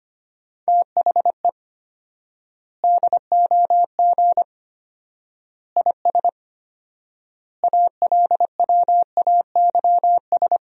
Morse code, heard as a keyed tone, about 25 wpm, 700 Hz. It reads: THE DOG IS ALWAYS
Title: 4x Word-Spacing - Sentences from Top 400 Words - 25wpm